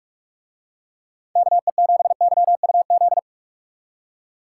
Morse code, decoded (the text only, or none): KE7YUZ